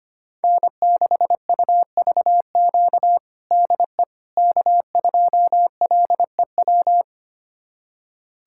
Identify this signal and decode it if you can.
Morse code, heard as a keyed tone, about 25 wpm, 700 Hz. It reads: N6UVQ DE K2LEW
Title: USA Callsign DE Callsign - 5 Characters - 25wpm